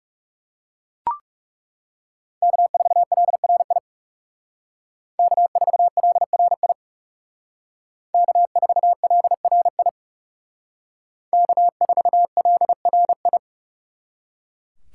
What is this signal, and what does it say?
Morse code, heard as a keyed tone, about 37 wpm, 700 Hz. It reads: K4LRI K4LRI K4LRI K4LRI
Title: Speed Racing - USA Call Signs 30wpm